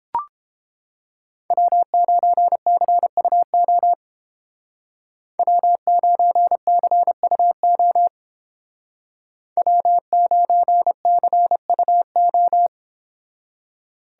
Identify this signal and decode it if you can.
Morse code, heard as a keyed tone, about 29 wpm, 700 Hz. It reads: W9CUO W9CUO W9CUO